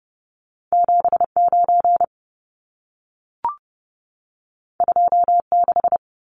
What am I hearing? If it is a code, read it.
Morse code, heard as a keyed tone, about 30 wpm, 700 Hz. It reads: 79 26
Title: Numbers - 2-Digit 30wpm